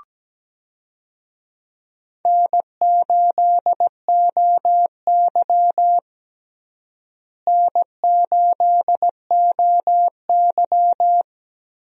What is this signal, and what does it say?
Morse code, heard as a keyed tone, about 17 wpm, 700 Hz. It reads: N8OY N8OY
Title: USA Call Signs - 4 Characters - Repeated 2x - 17wpm